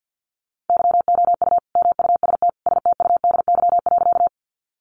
Morse code, heard as a keyed tone, about 50 wpm, 700 Hz. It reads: YOU MUST STUDY.
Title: Sentences from Top 300 Words 50wpm